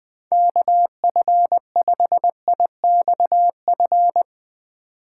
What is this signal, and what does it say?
Morse code, heard as a keyed tone, about 20 wpm, 700 Hz. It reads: KF5IXF
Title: USA Call Signs - 6 Characters - 3x Character Spacing - 20wpm